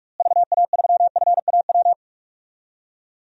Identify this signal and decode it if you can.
Morse code, heard as a keyed tone, about 43 wpm, 700 Hz. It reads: VA3FAW
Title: DXCC-100 Call Signs - 6 Characters - Repeated 3x - 45wpm